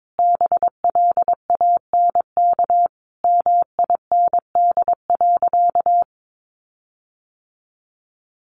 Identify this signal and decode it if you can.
Morse code, heard as a keyed tone, about 22 wpm, 700 Hz. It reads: BLANK MIND.
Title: Sentences from Top 1500 Words 22wpm